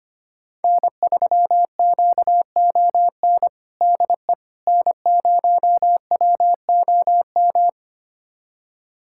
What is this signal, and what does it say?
Morse code, heard as a keyed tone, about 25 wpm, 700 Hz. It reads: N3QON DE N0WOM
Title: USA Callsign DE Callsign - 5 Characters - 25wpm